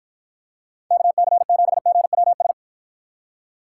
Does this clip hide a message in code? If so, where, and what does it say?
Yes: Morse code from 0.9 to 2.5 s, heard as a keyed tone, about 53 wpm, 700 Hz. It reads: KC7GWS